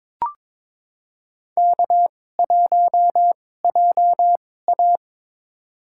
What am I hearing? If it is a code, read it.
Morse code, heard as a keyed tone, about 22 wpm, spaced as for 16 wpm, 700 Hz. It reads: K1JA
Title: USA Call Signs - 4 Characters - 2x Character Spacing - 22wpm